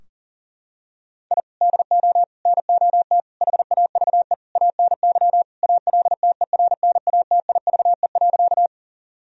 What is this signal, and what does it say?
Morse code, heard as a keyed tone, about 40 wpm, 700 Hz. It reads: I DO NOT HAVE ANY ALTERNATIVE.
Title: Sentences from Top 1000 Words - Review Part 4 - 40wpm